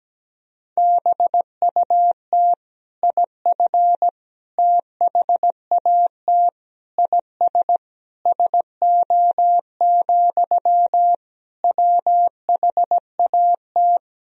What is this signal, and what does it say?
Morse code, heard as a keyed tone, about 17 wpm, 700 Hz. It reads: BUT IF THAT IS SO, WHAT
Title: Sentences from Top 200 Words and Abbreviations 17wpm